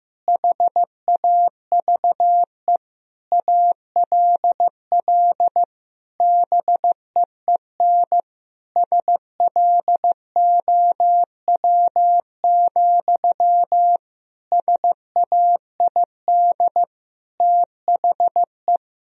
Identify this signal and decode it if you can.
Morse code, heard as a keyed tone, about 15 wpm, 700 Hz. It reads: HAVE ALL BEEN SLOW, SAID THE
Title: Sentences from Top 1000 Words - Review Part 2 - 15wpm